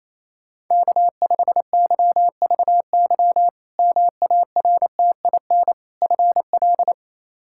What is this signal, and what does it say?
Morse code, heard as a keyed tone, about 28 wpm, 700 Hz. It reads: K5YVY MARTIN FL
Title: K1USN SST QSO Practice - 28wpm